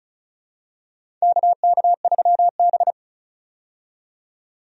Morse code, heard as a keyed tone, about 36 wpm, 700 Hz. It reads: KK3B
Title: USA Call Signs - 4 Characters - 35wpm